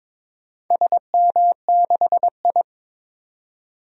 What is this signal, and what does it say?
Morse code, heard as a keyed tone, about 22 wpm, 700 Hz. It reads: SM6I
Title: DXCC-100 Call Signs - 4 Characters - 22wpm